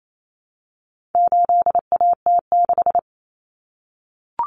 Morse code, heard as a keyed tone, about 28 wpm, 700 Hz. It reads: 8AT6
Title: Common Cut Numbers - 4-Digit 28wpm